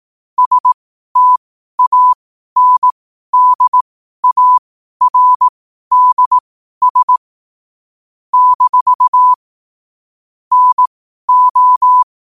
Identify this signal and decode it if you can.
Morse code, heard as a keyed tone, about 18 wpm, spaced as for 12 wpm, 1000 Hz. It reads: STANDARDS - NO
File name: qotdfiles / 12wpm